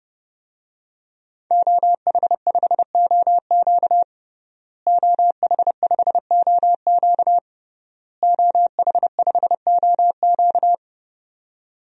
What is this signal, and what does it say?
Morse code, heard as a keyed tone, about 31 wpm, 700 Hz. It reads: OH5OQ OH5OQ OH5OQ
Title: DXCC-100 Call Signs - 5 Characters - Repeated 3x - 30wpm